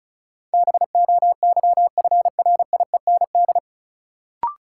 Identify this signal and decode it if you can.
Morse code, heard as a keyed tone, about 36 wpm, 700 Hz. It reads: BOYFRIEND
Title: Sentences from Top 1000 Words - Review Part 5 - New Word Drill - 35wpm